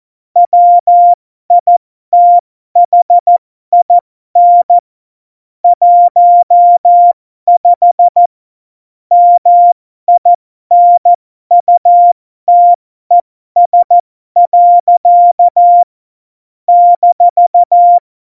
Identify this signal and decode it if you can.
Morse code, heard as a keyed tone, about 14 wpm, spaced as for 12 wpm, 700 Hz. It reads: WITHIN 15 MINUTES. -
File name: Morsekurs / Witze